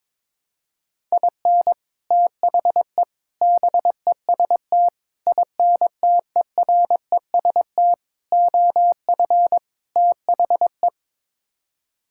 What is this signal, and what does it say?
Morse code, heard as a keyed tone, about 22 wpm, 700 Hz. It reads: IN THE BEST INTEREST OF THE
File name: practice / 22wpm / Sets of 6 Words - 22wpm